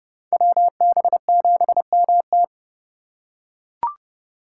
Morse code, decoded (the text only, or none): WB7MT